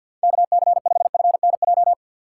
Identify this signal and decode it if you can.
Morse code, heard as a keyed tone, about 51 wpm, 700 Hz. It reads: KC5FNJ
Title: USA Call Signs - 6 Characters - 3x Character Spacing - 50wpm